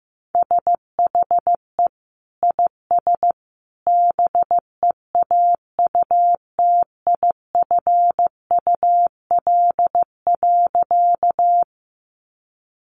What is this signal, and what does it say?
Morse code, heard as a keyed tone, about 15 wpm, 700 Hz. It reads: SHE IS BEAUTIFUL.
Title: Sentences from Top 900 Words 15wpm